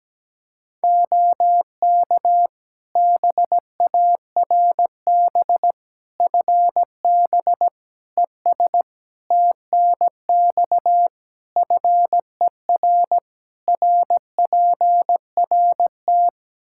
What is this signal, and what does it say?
Morse code, heard as a keyed tone, about 17 wpm, 700 Hz. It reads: OK BARB FB ES TNX FER RPRT
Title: Practice QSO - Exchange 2 - Info - 17wpm